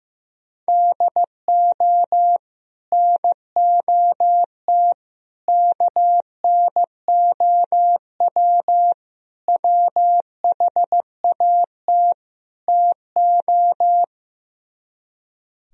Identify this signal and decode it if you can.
Morse code, heard as a keyed tone, about 16 wpm, 700 Hz. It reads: DO NOT KNOW WHAT TO
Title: Speed-Racing - Sets of 5 Words 15wpm